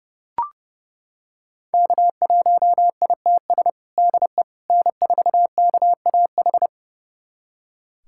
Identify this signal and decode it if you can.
Morse code, heard as a keyed tone, about 30 wpm, 700 Hz. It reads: K1ITS DE N4KAH
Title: USA Callsign DE Callsign - 5 Characters - 30wpm